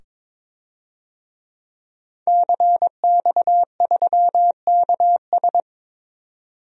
Morse code, heard as a keyed tone, about 22 wpm, 700 Hz. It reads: CX3KS
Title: DXCC-100 Call Signs - 5 Characters - 22wpm